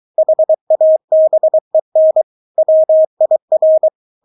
Morse code, heard as a keyed tone, about 23 wpm, 630 Hz. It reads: HABEN WIR
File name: Witz4 23wpm